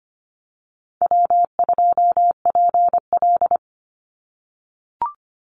Morse code, heard as a keyed tone, about 25 wpm, 700 Hz. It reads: W2PL
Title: Call Signs - 25wpm